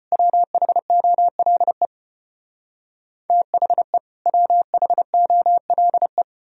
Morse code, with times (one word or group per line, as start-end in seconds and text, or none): 0.1-1.9 WHOLE
3.3-4.0 THE
4.3-6.2 WHOLE